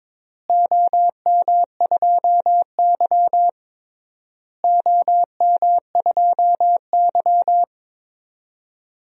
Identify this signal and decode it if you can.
Morse code, heard as a keyed tone, about 21 wpm, 700 Hz. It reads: OM2Y OM2Y
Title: DXCC-100 Call Signs - 4 Characters - Repeated 3x - 22wpm